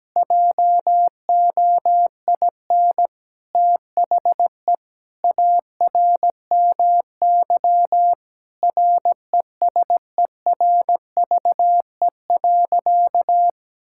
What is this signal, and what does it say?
Morse code, heard as a keyed tone, about 17 wpm, 700 Hz. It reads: JOIN THE ARMY RESERVE.
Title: Sentences from Top 1000 Words - Review Part 6 - 17wpm